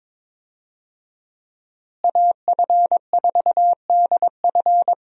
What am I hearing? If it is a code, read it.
Morse code, heard as a keyed tone, about 22 wpm, 700 Hz. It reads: AF4DF
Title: USA Call Signs - 5 Characters - Repeated 2x - 22wpm